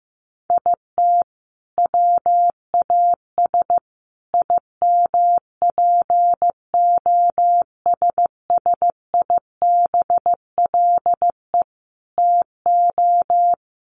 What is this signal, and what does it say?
Morse code, heard as a keyed tone, about 14 wpm, 700 Hz. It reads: IT WAS IMPOSSIBLE TO
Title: Sentences from Top 1100 Words 15wpm